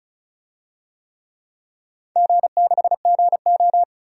Morse code, heard as a keyed tone, about 34 wpm, 700 Hz. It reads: G6GO
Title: DXCC-100 Call Signs - 4 Characters - Repeated 3x - 35wpm